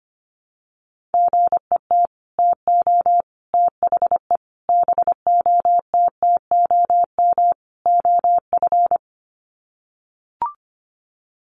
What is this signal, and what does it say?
Morse code, heard as a keyed tone, about 25 wpm, 700 Hz. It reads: GET TO THE BOTTOM OF
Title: Sets of 5 Words Encore 25wpm